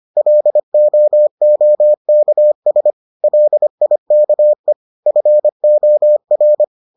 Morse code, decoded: LOOKS LIKE FOR

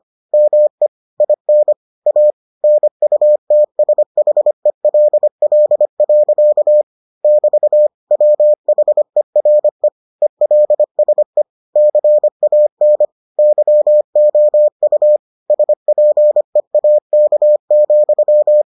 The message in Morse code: ME IN A NUTSHELL. <BT> WHERE ELSE CAN YOU SPEAK,

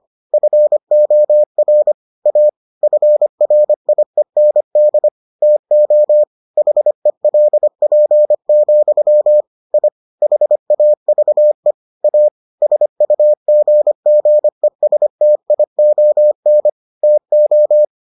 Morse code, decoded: FOR A FRIEND TO HELP, I HAVE A SUGGESTION TO

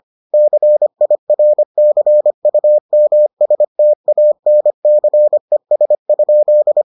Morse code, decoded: CIRCUMSTANCES?

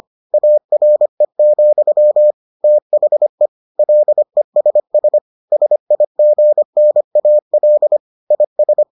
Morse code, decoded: ARE, THE LESS SIGNAL IS

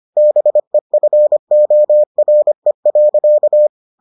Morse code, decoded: BEFORE.

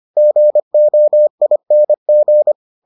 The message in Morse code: GOING

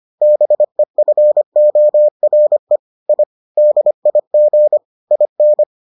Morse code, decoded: BEFORE I DIG IN

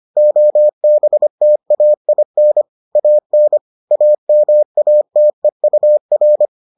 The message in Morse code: OBTAIN AN AMATEUR

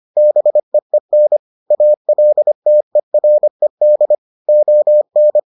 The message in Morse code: BEEN ALTERED ON

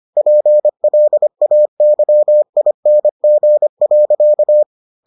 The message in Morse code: PLAYING.

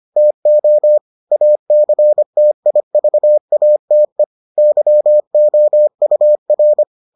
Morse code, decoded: TO ACTIVATE YOUR

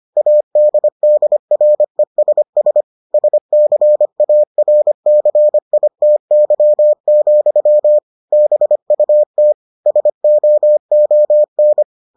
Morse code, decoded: ADDRESS SCARCITY, BUT SOON

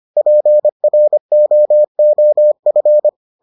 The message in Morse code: PROOF

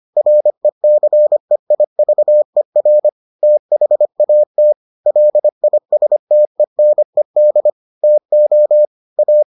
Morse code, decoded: RECEIVER THAT LISTENED TO A

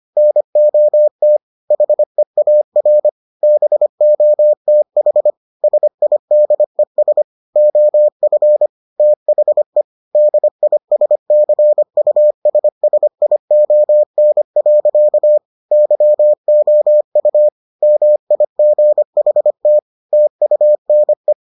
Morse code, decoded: NOT HEAR BOTH SIDES OF THE DISCUSSION. YOU MIGHT TUNE

